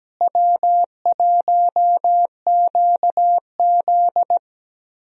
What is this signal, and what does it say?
Morse code, heard as a keyed tone, about 17 wpm, 700 Hz. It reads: W1QZ
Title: USA Call Signs - 4 Characters - Repeated 2x - 17wpm